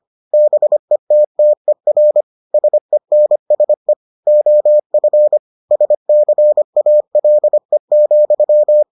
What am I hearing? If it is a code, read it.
Morse code, heard as a keyed tone, about 25 wpm, 600 Hz. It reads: BETTER SENSE OF SCALE,